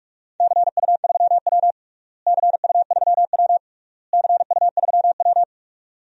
Morse code, heard as a keyed tone, about 41 wpm, 700 Hz. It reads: CU3W CU3W CU3W
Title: DXCC-100 Call Signs - 4 Characters - Repeated 3x - 45wpm